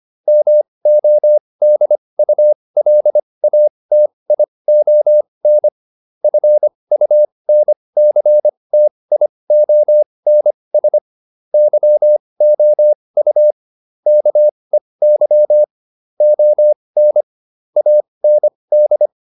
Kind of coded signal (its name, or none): Morse code